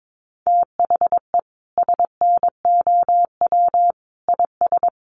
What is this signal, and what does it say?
Morse code, heard as a keyed tone, about 22 wpm, 700 Hz. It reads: THE SNOW IS
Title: Sentences from Top 500 Words 22wpm